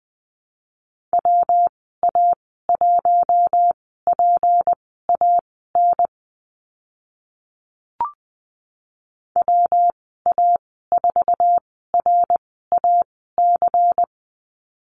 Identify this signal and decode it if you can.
Morse code, heard as a keyed tone, about 20 wpm, spaced as for 14 wpm, 700 Hz. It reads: WA1PAN WA4RAC
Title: 167 Lesson 27g - Number 7 - Call Signs 20wpm